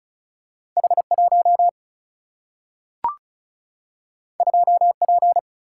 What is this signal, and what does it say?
Morse code, heard as a keyed tone, about 36 wpm, 700 Hz. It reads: H 1 2 P
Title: Single-Letter Single-Number 35wpm